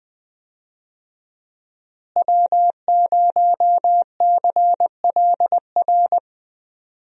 Morse code, heard as a keyed tone, about 20 wpm, 700 Hz. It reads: W0CLR
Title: USA Call Signs - 5 Characters - Repeated 3x - 20wpm